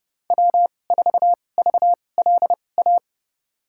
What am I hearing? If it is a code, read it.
Morse code, heard as a keyed tone, about 31 wpm, spaced as for 22 wpm, 700 Hz. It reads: W4VLA
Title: USA Call Signs - 5 Characters - 2x Character Spacing - 30wpm